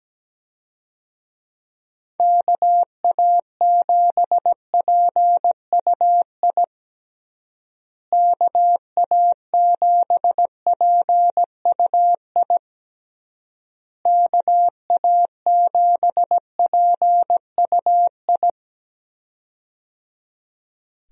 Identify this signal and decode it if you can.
Morse code, heard as a keyed tone, about 17 wpm, 700 Hz. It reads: KA7PUI KA7PUI KA7PUI
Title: USA Call Signs - 6 Characters - Repeated 3x - 17wpm